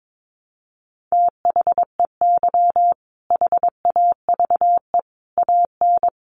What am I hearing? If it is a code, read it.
Morse code, heard as a keyed tone, about 22 wpm, 700 Hz. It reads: THEY HAVE AN
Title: Sentences from Top 1500 Words 22wpm